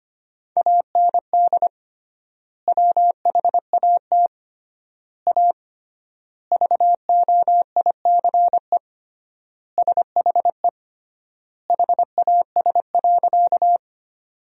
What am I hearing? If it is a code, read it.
Morse code, heard as a keyed tone, about 25 wpm, 700 Hz. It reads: AND WHAT A VOICE SHE HAS.
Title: 3x Word-Spacing - Sentences from Top 500 Words - 25wpm